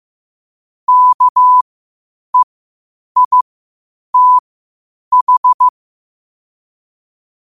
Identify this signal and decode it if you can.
Morse code, heard as a keyed tone, about 15 wpm, spaced as for 8 wpm, 1000 Hz. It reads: KEITH
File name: qotdfiles / 8wpm